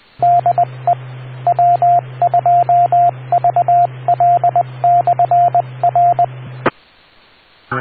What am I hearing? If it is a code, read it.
Morse code, heard as a keyed tone, about 20 wpm, 700 Hz, with radio noise. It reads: DE W2VL/R